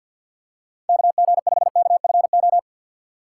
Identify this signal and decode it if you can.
Morse code, heard as a keyed tone, about 53 wpm, 700 Hz. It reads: KG5KFO
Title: USA Call Signs - 6 Characters - 3x Character Spacing - 50wpm